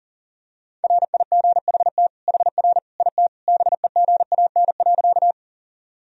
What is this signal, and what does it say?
Morse code, heard as a keyed tone, about 40 wpm, 700 Hz. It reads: RIGHT HR IT BEGAN.
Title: Sentences from Top 300 Words and Abbreviations 40wpm